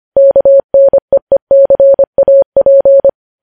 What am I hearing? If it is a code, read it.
Morse code, heard as a keyed tone, about 25 wpm, 550 Hz. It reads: KNEECAP
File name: podcast / News headlines-1x-25x25